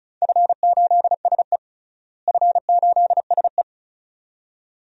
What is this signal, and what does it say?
Morse code, heard as a keyed tone, about 36 wpm, 700 Hz. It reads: F8SE F8SE
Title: DXCC-100 Call Signs - 4 Characters - Repeated 2x - 35wpm